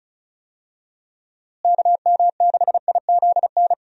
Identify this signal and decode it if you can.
Morse code, heard as a keyed tone, about 34 wpm, 700 Hz. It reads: KM6IZN
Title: USA Call Signs - 6 Characters - Repeated 3x - 35wpm